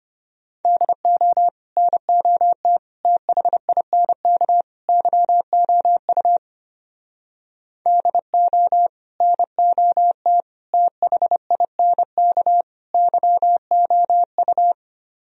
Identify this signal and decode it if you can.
Morse code, heard as a keyed tone, about 33 wpm, 700 Hz. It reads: DO NOT THINK YOU DO NOT THINK YOU